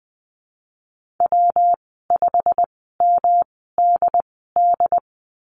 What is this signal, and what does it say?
Morse code, heard as a keyed tone, about 20 wpm, spaced as for 15 wpm, 700 Hz. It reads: W5MDD
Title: 122 Lesson 20g - Number 3 - Call Signs 20wpm